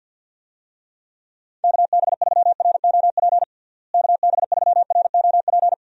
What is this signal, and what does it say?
Morse code, heard as a keyed tone, about 52 wpm, 700 Hz. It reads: KB3ROP KB3ROP
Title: USA Call Signs - 6 Characters - Repeated 3x - 50wpm